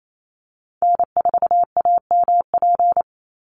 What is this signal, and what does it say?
Morse code, heard as a keyed tone, about 28 wpm, 700 Hz. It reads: N4AMP
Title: USA Call Signs 28wpm